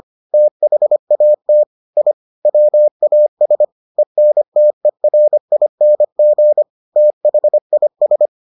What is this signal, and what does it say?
Morse code, heard as a keyed tone, about 25 wpm, 600 Hz. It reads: THAT I WAS ENTERING THIS